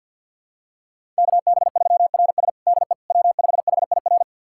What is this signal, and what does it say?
Morse code, heard as a keyed tone, about 52 wpm, 700 Hz. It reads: KB3RS DE W5HIR
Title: USA Callsign DE Callsign 50wpm